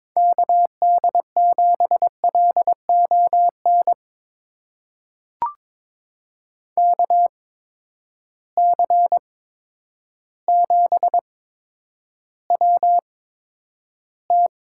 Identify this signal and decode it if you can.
Morse code, heard as a keyed tone, about 22 wpm, spaced as for 10 wpm, 700 Hz. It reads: KD7LON KC7WT
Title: USA Call Signs - 6 Characters - 8x Character Spacing - 22wpm